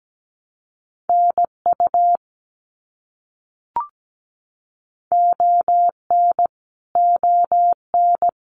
Morse code, heard as a keyed tone, about 17 wpm, 700 Hz. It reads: NU ON ON
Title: Canadian Provinces and Territories - Repeated 3x - 17wpm